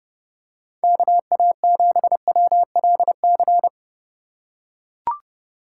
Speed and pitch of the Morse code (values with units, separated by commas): 32 wpm, 700 Hz